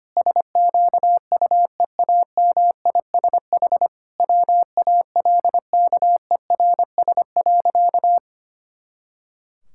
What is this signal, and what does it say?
Morse code, heard as a keyed tone, about 25 wpm, 700 Hz. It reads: SQUEAMISH WALKERS.
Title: Pangram Challenge v2 - 25wpm